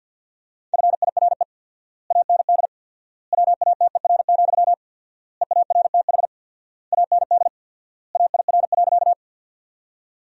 Morse code, heard as a keyed tone, about 50 wpm, 700 Hz. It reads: FIRE AND WATER, EARTH AND AIR.
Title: 4x Word-Spacing - Sentences from Top 400 Words - 50wpm